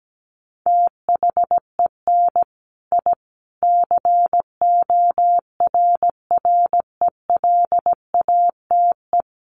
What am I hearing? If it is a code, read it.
Morse code, heard as a keyed tone, about 17 wpm, 700 Hz. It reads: THEN I CORRELATE
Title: A Sound Of Thunder 17wpm